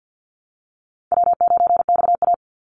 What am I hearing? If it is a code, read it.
Morse code, heard as a keyed tone, about 50 wpm, 700 Hz. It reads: W9XA
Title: Call Signs - 50wpm